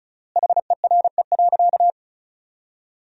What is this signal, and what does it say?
Morse code, heard as a keyed tone, about 36 wpm, 700 Hz. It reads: HERE.
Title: 3x Word-Spacing - Sentences from Top 200 Words - 35wpm